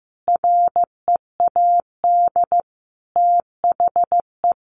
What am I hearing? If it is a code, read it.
Morse code, heard as a keyed tone, about 15 wpm, 700 Hz. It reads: READ THE
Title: Sentences from Top 1300 Words 15wpm